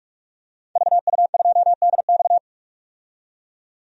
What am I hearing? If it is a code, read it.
Morse code, heard as a keyed tone, about 46 wpm, 700 Hz. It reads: VU2DX
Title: DXCC-100 Call Signs - 5 Characters - 45wpm